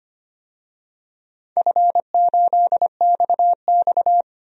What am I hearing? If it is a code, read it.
Morse code, heard as a keyed tone, about 25 wpm, 700 Hz. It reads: F8XX
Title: DXCC-100 Call Signs - 4 Characters - 25wpm